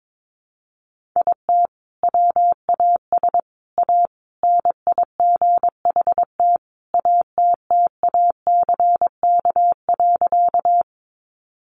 Morse code, heard as a keyed tone, about 22 wpm, 700 Hz. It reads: IT WAS A NIGHT ATTACK.
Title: Sentences from Top 1500 Words 22wpm